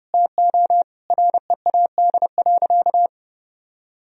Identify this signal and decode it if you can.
Morse code, heard as a keyed tone, about 32 wpm, 700 Hz. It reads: TO READ.
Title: Sentences from Top 1000 Words - Review Part 1 - 30wpm